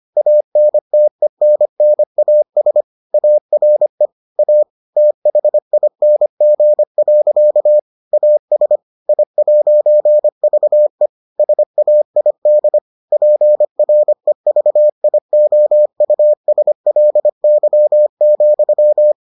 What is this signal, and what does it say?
Morse code, heard as a keyed tone, about 25 wpm, 600 Hz. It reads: ANTENNAS ARE A THING. AS I'VE SAID PREVIOUSLY,